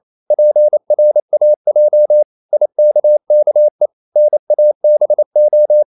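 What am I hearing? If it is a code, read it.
Morse code, heard as a keyed tone, about 28 wpm, 600 Hz. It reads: PRAJ IKKE NABO
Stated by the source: Danish short words sent in Morse at 28 wpm, from CW med Gnister.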